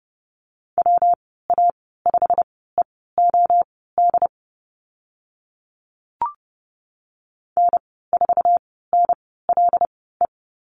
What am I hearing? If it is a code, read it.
Morse code, heard as a keyed tone, about 30 wpm, spaced as for 17 wpm, 700 Hz. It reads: WA5EOD N4NLE
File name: Lesson 19g - Letter w - Call Signs 30wpm